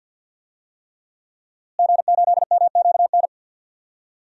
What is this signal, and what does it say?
Morse code, heard as a keyed tone, about 47 wpm, 700 Hz. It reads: G8MQN
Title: DXCC-100 Call Signs - 5 Characters - Repeated 3x - 50wpm